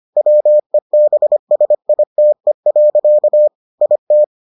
Morse code, heard as a keyed tone, about 25 wpm, 600 Hz. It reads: WEBSITE. IT